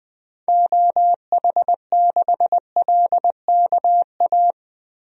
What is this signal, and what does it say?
Morse code, heard as a keyed tone, about 20 wpm, 700 Hz. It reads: OH6LKA
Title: DXCC-100 Call Signs - 6 Characters - Repeated 3x - 20wpm